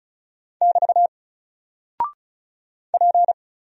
Morse code, heard as a keyed tone, about 34 wpm, 700 Hz. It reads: <BT> P
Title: Single Character (Letter:Number:Punctuation:Prosign) Alt 35wpm